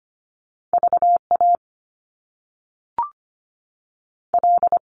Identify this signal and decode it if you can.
Morse code, heard as a keyed tone, about 25 wpm, 700 Hz. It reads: VA L
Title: US States 25wpm